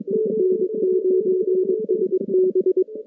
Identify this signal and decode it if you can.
Morse code, heard as a keyed tone, about 22 wpm, 370 Hz. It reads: N9SB
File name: contests / CW-SS-k5tr- -2200